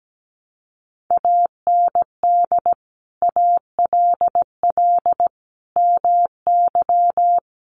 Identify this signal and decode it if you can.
Morse code, heard as a keyed tone, about 17 wpm, 700 Hz. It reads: AND ALL MY
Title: Sentences from Top 1500 Words 17wpm